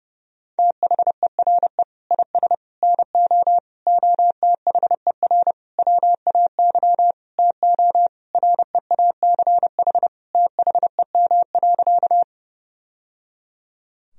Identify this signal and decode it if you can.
Morse code, heard as a keyed tone, about 32 wpm, 700 Hz. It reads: THERE IS NO OTHER WAY TO REACH THEM.
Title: Sentences from Top 1000 Words - Review Part 2 - 30wpm